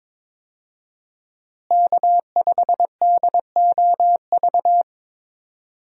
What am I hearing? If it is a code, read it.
Morse code, heard as a keyed tone, about 22 wpm, 700 Hz. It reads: K5DOV
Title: USA Call Signs - 5 Characters - Repeated 3x - 22wpm